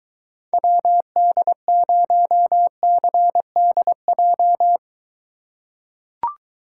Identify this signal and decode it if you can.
Morse code, heard as a keyed tone, about 23 wpm, 700 Hz. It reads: WD0CDJ